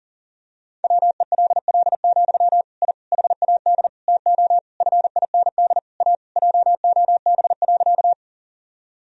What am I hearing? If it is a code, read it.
Morse code, heard as a keyed tone, about 37 wpm, 700 Hz. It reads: WELL, I HAD TO FIND A JOB.
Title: Sentences from Top 1000 Words - Review Part 3 - 40wpm